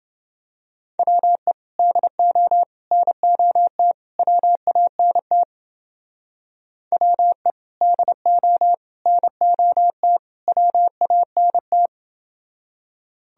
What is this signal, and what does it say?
Morse code, heard as a keyed tone, about 29 wpm, 700 Hz. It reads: WE DO NOT WANT WE DO NOT WANT